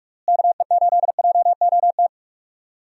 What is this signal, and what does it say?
Morse code, heard as a keyed tone, about 47 wpm, 700 Hz. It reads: KE8JOT
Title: USA Call Signs - 6 Characters - Repeated 3x - 45wpm